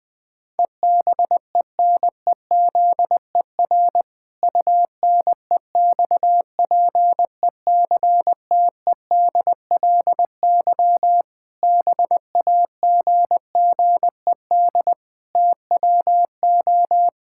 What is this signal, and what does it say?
Morse code, heard as a keyed tone, about 19 wpm, 700 Hz. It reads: EBENEZER UNEXPECTEDLY BAGGED TWO
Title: Pangram Challenge v2 - 20wpm